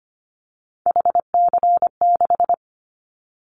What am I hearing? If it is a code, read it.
Morse code, heard as a keyed tone, about 25 wpm, 700 Hz. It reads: HC6
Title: Callsign Prefixes for Top 100 DXCC - 25wpm